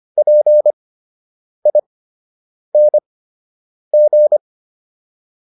Morse code, heard as a keyed tone, about 25 wpm, spaced as for 8 wpm, 600 Hz. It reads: PING